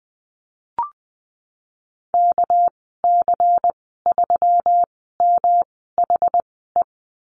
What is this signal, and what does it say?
Morse code, heard as a keyed tone, about 20 wpm, spaced as for 14 wpm, 700 Hz. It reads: KC3MHE
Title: 213 Lesson 34g - Number 8 - Call Signs 20wpm